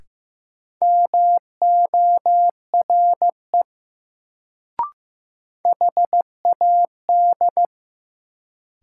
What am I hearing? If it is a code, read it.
Morse code, heard as a keyed tone, about 15 wpm, 700 Hz. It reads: MORE HAD
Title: Sentences from Top 100 Words - New Word Drill - 15wpm